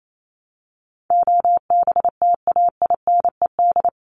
Morse code, heard as a keyed tone, about 28 wpm, 700 Hz. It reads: OBTAINED
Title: ED Suffix - Top 100 Words with Base Words 28wpm